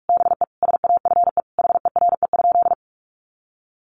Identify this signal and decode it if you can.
Morse code, heard as a keyed tone, about 44 wpm, 700 Hz. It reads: BE SAFE HERE?
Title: Sentences from Top 1300 Words 45wpm